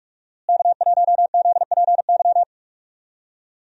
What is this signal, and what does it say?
Morse code, heard as a keyed tone, about 46 wpm, 700 Hz. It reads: K1ZPY
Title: USA Call Signs - 5 Characters - Repeated 3x - 45wpm